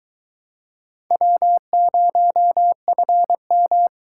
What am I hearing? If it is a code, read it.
Morse code, heard as a keyed tone, about 22 wpm, 700 Hz. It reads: W0FM